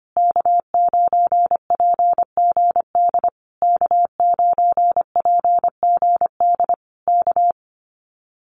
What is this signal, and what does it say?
Morse code, heard as a keyed tone, about 25 wpm, 700 Hz. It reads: K9PGD K9PGD K
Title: Common QSO Phrases Encore 25wpm